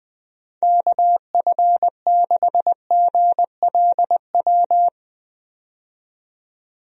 Morse code, heard as a keyed tone, about 20 wpm, 700 Hz. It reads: KF6GLW
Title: USA Call Signs - 6 Characters - 20wpm